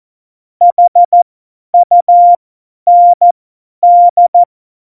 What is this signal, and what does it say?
Morse code, heard as a keyed tone, about 14 wpm, spaced as for 10 wpm, 700 Hz. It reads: HUND
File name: Morsekurs / Witze / Witz 25. Dezember, 50 F70